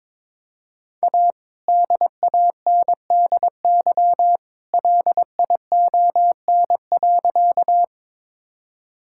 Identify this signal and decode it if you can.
Morse code, heard as a keyed tone, about 22 wpm, 690 Hz. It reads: A DANDY LION.
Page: morse-code-ninja-practice-22wpm